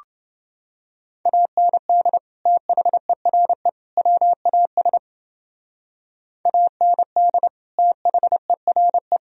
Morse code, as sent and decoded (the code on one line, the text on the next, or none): .- -. -.. / - .... .-. / .-- .- ... / .- -. -.. / - .... .-. .
AND THERE WAS AND THERE